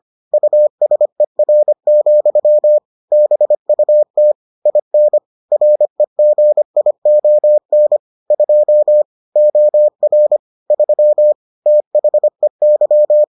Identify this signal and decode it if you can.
Morse code, heard as a keyed tone, about 25 wpm, 600 Hz. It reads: USER, BUT IN REGION 2 OR 3 THEY